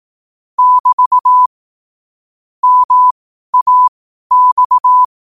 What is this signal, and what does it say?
Morse code, heard as a keyed tone, about 18 wpm, spaced as for 12 wpm, 1000 Hz. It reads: <BT> MAX
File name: qotdfiles / 12wpm